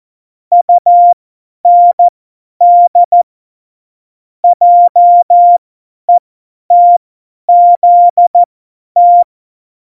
Morse code, heard as a keyed tone, about 14 wpm, spaced as for 10 wpm, 700 Hz. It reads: UND JETZT